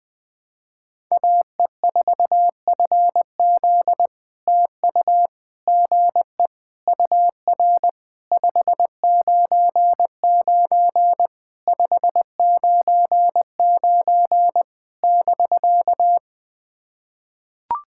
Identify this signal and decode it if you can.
Morse code, heard as a keyed tone, about 18 wpm, 700 Hz. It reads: AE4FZ TU GE UR 599 599 <BK>
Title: POTA QSO Practice - 20wpm